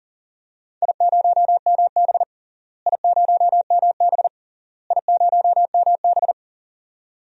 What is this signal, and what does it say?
Morse code, heard as a keyed tone, about 40 wpm, 700 Hz. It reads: I0MB I0MB I0MB
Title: DXCC-100 Call Signs - 4 Characters - Repeated 3x - 40wpm